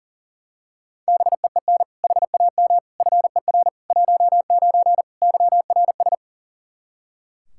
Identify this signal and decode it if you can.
Morse code, heard as a keyed tone, about 40 wpm, 700 Hz. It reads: BEEN HAM FER 19 YRS
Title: Practice QSO - Exchange 3 - More Info - 40wpm